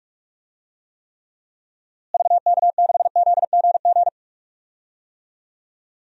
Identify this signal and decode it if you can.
Morse code, heard as a keyed tone, about 46 wpm, 700 Hz. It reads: VK6ZGG
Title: DXCC-100 Call Signs - 6 Characters - Repeated 2x - 45wpm